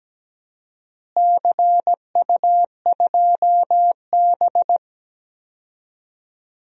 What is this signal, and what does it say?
Morse code, heard as a keyed tone, about 17 wpm, 700 Hz. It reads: CU2B
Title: DXCC-100 Call Signs - 4 Characters - Repeated 3x - 17wpm